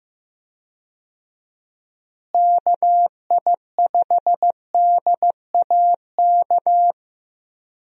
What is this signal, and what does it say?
Morse code, heard as a keyed tone, about 15 wpm, 700 Hz. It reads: KI5DAK
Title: USA Call Signs - 6 Characters - Repeated 3x - 15wpm